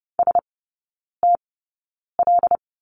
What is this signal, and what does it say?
Morse code, heard as a keyed tone, about 30 wpm, spaced as for 9 wpm, 690 Hz. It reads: STL
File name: Lesson 16e - Letter c - Three Characters 30wpm